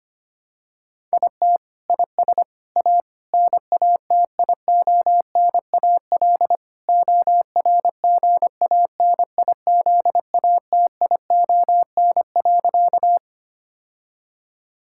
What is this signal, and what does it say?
Morse code, heard as a keyed tone, about 25 wpm, 700 Hz. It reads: IT IS A NATIONAL ORGANIZATION.
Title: Sentences from Top 1000 Words - Review Part 5 - 25wpm